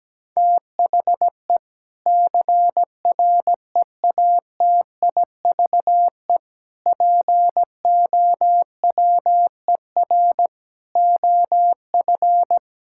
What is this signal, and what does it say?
Morse code, heard as a keyed tone, about 17 wpm, 700 Hz. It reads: THE CREATIVE POWER OF
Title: Sentences from Top 1000 Words - Review Part 5 - 17wpm